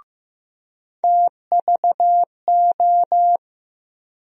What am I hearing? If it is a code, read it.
Morse code, heard as a keyed tone, about 15 wpm, 700 Hz. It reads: TVO
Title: USA Call Sign Postfixes - 3 Letters 15wpm